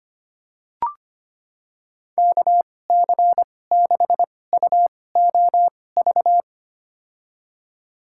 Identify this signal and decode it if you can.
Morse code, heard as a keyed tone, about 25 wpm, spaced as for 18 wpm, 700 Hz. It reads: KC6UOV
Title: USA Call Signs - 6 Characters - 2x Character Spacing - 25wpm